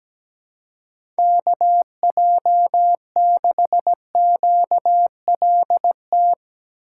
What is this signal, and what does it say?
Morse code, heard as a keyed tone, about 17 wpm, 700 Hz. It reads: KJ6QLT
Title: USA Call Signs - 6 Characters - 2x Character Spacing - 17wpm